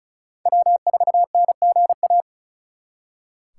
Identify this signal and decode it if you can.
Morse code, heard as a keyed tone, about 45 wpm, 700 Hz. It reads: W4NGA